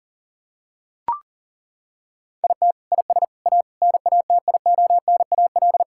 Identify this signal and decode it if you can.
Morse code, heard as a keyed tone, about 40 wpm, 700 Hz. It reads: IT IS A NATIONAL
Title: Sentences from Top 1000 Words - Review Part 5 - 40wpm